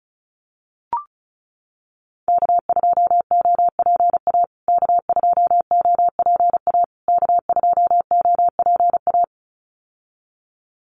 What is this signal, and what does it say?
Morse code, heard as a keyed tone, about 34 wpm, 700 Hz. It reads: K2OPA K2OPA K2OPA
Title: USA Call Signs Repeated 3x 35wpm